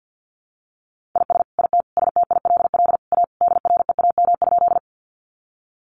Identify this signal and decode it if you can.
Morse code, heard as a keyed tone, about 50 wpm, 700 Hz. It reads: IS IT STILL A DREAM?
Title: Sentences from Top 1500 Words 50wpm